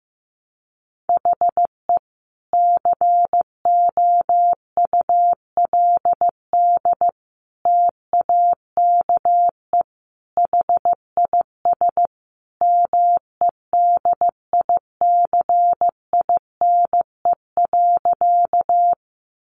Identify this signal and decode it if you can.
Morse code, heard as a keyed tone, about 15 wpm, 700 Hz. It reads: HE COULD TAKE HIS MEDICINE.
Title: Sentences from Top 900 Words 15wpm